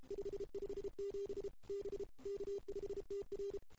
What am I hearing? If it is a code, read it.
Morse code, heard as a keyed tone, about 25 wpm, 410 Hz. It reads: 557 B K5TR
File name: CW / 2008-ss-cw-k5tr-op-n5rz -02:00